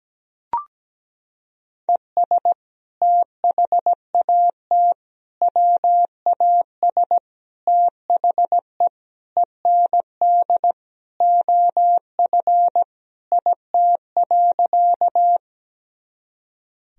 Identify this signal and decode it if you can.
Morse code, heard as a keyed tone, about 17 wpm, 700 Hz. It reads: ES THAT WAS THE END OF IT.
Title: Sentences from Top 200 Words and Abbreviations 17wpm